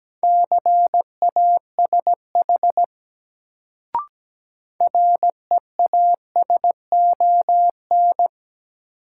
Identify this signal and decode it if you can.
Morse code, heard as a keyed tone, about 17 wpm, 700 Hz. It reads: CASH REASON
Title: Sentences from Top 700 Words - New Word Drill - 17wpm